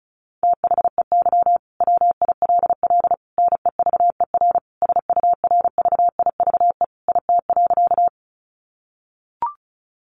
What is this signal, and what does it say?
Morse code, heard as a keyed tone, about 34 wpm, 700 Hz. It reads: THEY WILL NEVER SURVIVE IT.
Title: Sentences from Top 1300 Words 35wpm